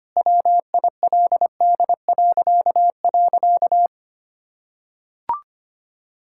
Morse code, heard as a keyed tone, about 25 wpm, 700 Hz. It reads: WILD..
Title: Pangram Challenge v2 - 25wpm